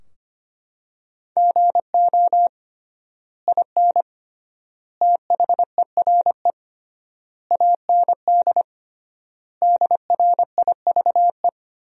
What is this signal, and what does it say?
Morse code, heard as a keyed tone, about 25 wpm, 700 Hz. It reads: GO IN THERE AND DRIVE
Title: 3x Word-Spacing - Sentences from Top 500 Words - 25wpm